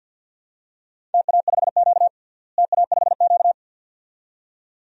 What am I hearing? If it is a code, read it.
Morse code, heard as a keyed tone, about 52 wpm, 700 Hz. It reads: TA5Q TA5Q
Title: DXCC-100 Call Signs - 4 Characters - Repeated 2x - 50wpm